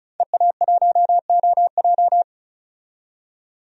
Morse code, heard as a keyed tone, about 36 wpm, 700 Hz. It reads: EA1OJ
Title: DXCC-100 Call Signs - 5 Characters - 35wpm